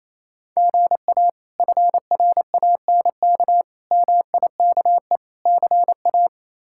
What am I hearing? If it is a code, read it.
Morse code, heard as a keyed tone, about 28 wpm, 700 Hz. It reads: GA FRANK MIKE CA
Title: K1USN SST QSO Practice - 28wpm